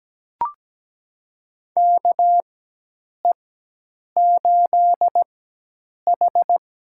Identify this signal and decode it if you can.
Morse code, heard as a keyed tone, about 17 wpm, spaced as for 8 wpm, 700 Hz. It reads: KE8H
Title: USA Call Signs - 4x Character Spacing - 17wpm